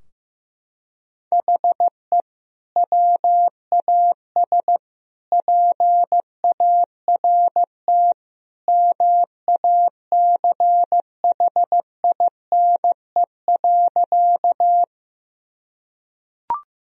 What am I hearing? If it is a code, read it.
Morse code, heard as a keyed tone, about 14 wpm, 700 Hz. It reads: HE WAS PART MACHINE.
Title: Sentences from Top 1000 Words - Review Part 2 - 15wpm